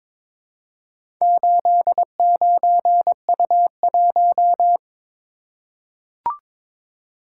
Morse code, heard as a keyed tone, about 22 wpm, 700 Hz. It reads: 89U1
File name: practice / 22wpm / Four Letter-Number Combinations - 22wpm